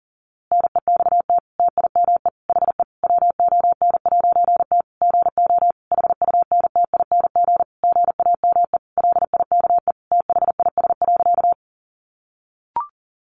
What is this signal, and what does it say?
Morse code, heard as a keyed tone, about 40 wpm, 700 Hz. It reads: NEXT TIME HE WON'T GO HUNTING GAME LIKE THIS.
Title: A Sound Of Thunder 40wpm